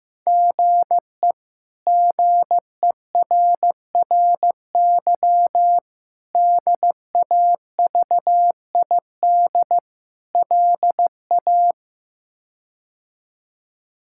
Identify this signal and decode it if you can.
Morse code, heard as a keyed tone, about 16 wpm, 700 Hz. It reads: GE GERRY DAVID LA
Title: K1USN SST QSO Practice - 15wpm